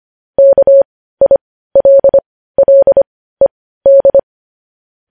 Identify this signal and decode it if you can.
Morse code, heard as a keyed tone, about 25 wpm, spaced as for 15 wpm, 550 Hz. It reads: KILLED
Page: news-headlines-in-morse-code-at-15-wpm